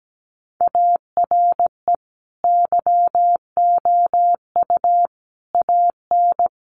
Morse code, heard as a keyed tone, about 17 wpm, 700 Hz. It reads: ARE YOU AN
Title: Sentences from Top 1200 Words 17wpm